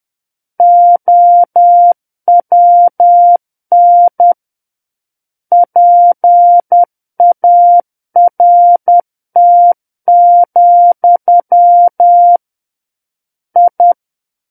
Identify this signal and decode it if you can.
Morse code, heard as a keyed tone, about 10 wpm, 700 Hz. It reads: OWN PART, I
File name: Book One / 01-Chapter 1 - 10 WPM